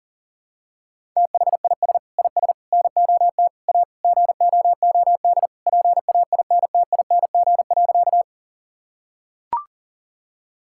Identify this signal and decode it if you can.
Morse code, heard as a keyed tone, about 40 wpm, 700 Hz. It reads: THIS IS NOT A GOOD PAINTING.
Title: Sentences from Top 1000 Words - Review Part 6 - 40wpm